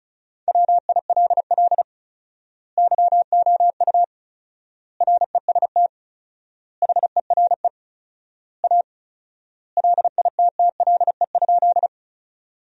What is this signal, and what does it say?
Morse code, heard as a keyed tone, about 34 wpm, 700 Hz. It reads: WILL YOU REST HERE A LITTLE?
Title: 4x Word-Spacing - Sentences from Top 500 Words - 35wpm